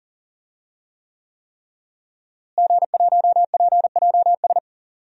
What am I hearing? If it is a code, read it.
Morse code, heard as a keyed tone, about 40 wpm, 700 Hz. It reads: G1PJS
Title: DXCC-100 Call Signs - 5 Characters - Repeated 3x - 40wpm